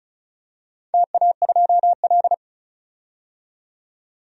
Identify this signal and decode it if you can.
Morse code, heard as a keyed tone, about 34 wpm, 700 Hz. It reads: TA2L
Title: DXCC-100 Call Signs - 4 Characters - 35wpm